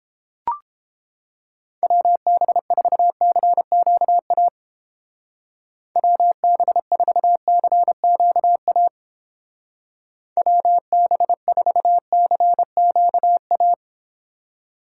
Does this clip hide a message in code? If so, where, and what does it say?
Yes: Morse code from 1.8 to 13.7 s, heard as a keyed tone, about 30 wpm, 700 Hz. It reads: WB4CQA WB4CQA WB4CQA